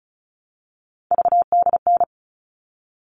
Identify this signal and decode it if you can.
Morse code, heard as a keyed tone, about 34 wpm, 700 Hz. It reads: VDN
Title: All Cut Numbers - 3-Digit 35wpm